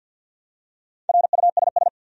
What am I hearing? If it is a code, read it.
Morse code, heard as a keyed tone, about 52 wpm, 700 Hz. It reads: RUSS
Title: USA Common Names - 50wpm